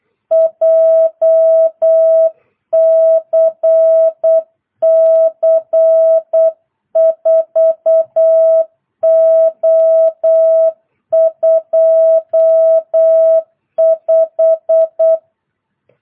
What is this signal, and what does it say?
Morse code, heard as a keyed tone, about 8 wpm, 660 Hz. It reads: JCC 4O25